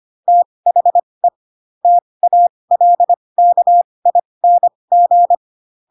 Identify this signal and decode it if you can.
Morse code, heard as a keyed tone, about 25 wpm, spaced as for 20 wpm, 700 Hz. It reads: THE TALKING
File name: Pinocchio / Pinocchio ch4 15 25 ik1hge